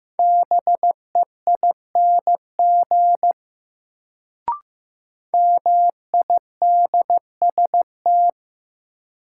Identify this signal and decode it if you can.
Morse code, heard as a keyed tone, about 15 wpm, 700 Hz. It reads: BEING MIDST
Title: Sets of 4 Words - New Word Drill - 15wpm